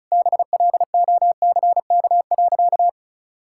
Morse code, heard as a keyed tone, about 36 wpm, 700 Hz. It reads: BLOCK.
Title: Sentences from Top 1600 Words 35wpm